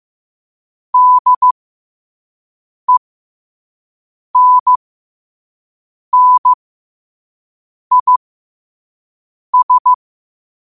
Morse code, heard as a keyed tone, about 15 wpm, spaced as for 5 wpm, 1000 Hz. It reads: DENNIS
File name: qotdfiles / 5wpm